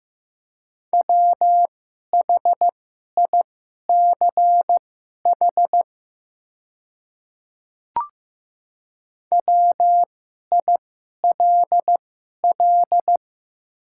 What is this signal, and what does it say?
Morse code, heard as a keyed tone, about 15 wpm, spaced as for 11 wpm, 700 Hz. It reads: WHICH WILL
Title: 2x Character Spacing - Top 100 Words - 15wpm